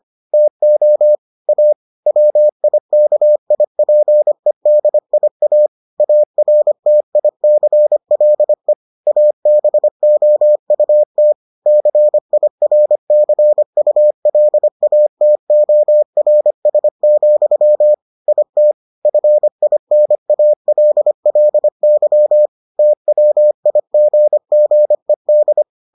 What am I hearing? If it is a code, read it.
Morse code, heard as a keyed tone, about 25 wpm, 600 Hz. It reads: TO A WIKIPEDIA ARTICLE ABOUT CIRCULATORS, IT FINALLY TWIGGED